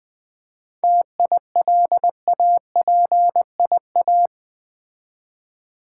Morse code, heard as a keyed tone, about 21 wpm, 700 Hz. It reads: TILAPIA
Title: Difficult Words with Hints - 20wpm